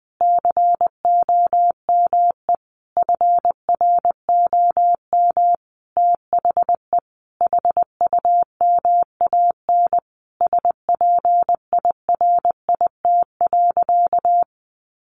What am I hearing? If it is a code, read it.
Morse code, heard as a keyed tone, about 20 wpm, 700 Hz. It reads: COME FROM THE HUMAN SPIRIT.
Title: Sentences from Top 1400 Words 20wpm